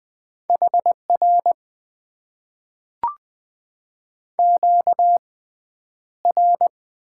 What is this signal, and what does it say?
Morse code, heard as a keyed tone, about 20 wpm, spaced as for 10 wpm, 700 Hz. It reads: HR QR
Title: Top 90 QSO Elements - 6x Character Spacing - 20wpm